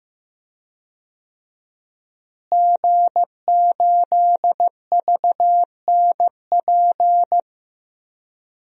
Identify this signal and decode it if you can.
Morse code, heard as a keyed tone, about 15 wpm, 700 Hz. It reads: G8VNP
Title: DXCC-100 Call Signs - 5 Characters - Repeated 3x - 15wpm